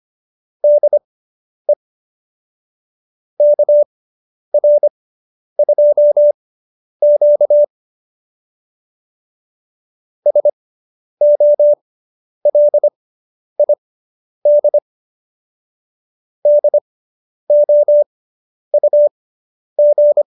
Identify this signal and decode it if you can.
Morse code, heard as a keyed tone, about 25 wpm, spaced as for 10 wpm, 590 Hz. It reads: DE KR2Q SOLID DOUG
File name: QSO220 10